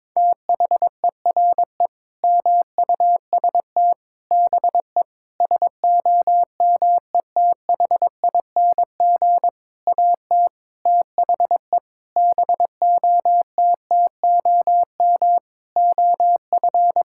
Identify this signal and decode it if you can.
Morse code, heard as a keyed tone, about 22 wpm, 700 Hz. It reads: THERE MUST BE SOMETHING AT THE BOTTOM OF
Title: Sentences from Top 1000 Words - Review Part 6 - 22wpm